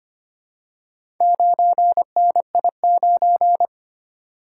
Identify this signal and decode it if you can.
Morse code, heard as a keyed tone, about 25 wpm, 700 Hz. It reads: 9NI9
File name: practice / 25wpm / Four Letter-Number Combinations - 25wpm